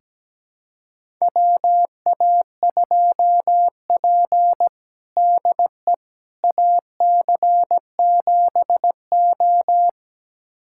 Morse code, heard as a keyed tone, about 17 wpm, 700 Hz. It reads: WA2P DE AC7O
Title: USA Callsign DE Callsign - 4 Characters - 17wpm